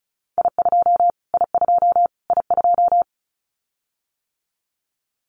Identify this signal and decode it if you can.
Morse code, heard as a keyed tone, about 34 wpm, 700 Hz. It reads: I2 I2 I2
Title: Callsign Prefixes for Top 25 DXCC - Repeated 3x - 35wpm